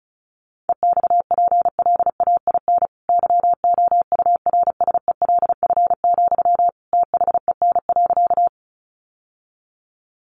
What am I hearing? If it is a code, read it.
Morse code, heard as a keyed tone, about 34 wpm, 700 Hz. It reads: EXPLAIN YOURSELF, THEN.
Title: Sentences from Top 1100 Words 35wpm